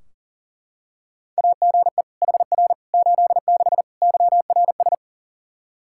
Morse code, heard as a keyed tone, about 40 wpm, 700 Hz. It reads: AGE HR 86 YRS
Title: Practice QSO - Exchange 3 - More Info - 40wpm